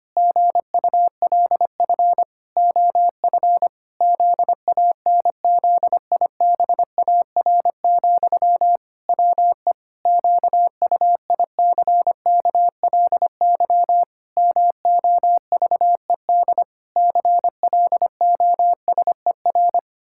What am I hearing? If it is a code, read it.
Morse code, heard as a keyed tone, about 25 wpm, 700 Hz. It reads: GULF OF ZANZIBAR, WE QUICKLY MOVED CLOSER
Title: Pangram Challenge v2 - 25wpm